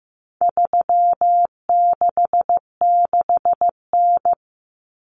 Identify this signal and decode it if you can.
Morse code, heard as a keyed tone, about 15 wpm, 700 Hz. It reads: 366N
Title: Common Cut Numbers - 4-Digit 15wpm